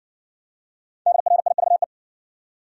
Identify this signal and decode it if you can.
Morse code, heard as a keyed tone, about 58 wpm, 700 Hz. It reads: DRIVE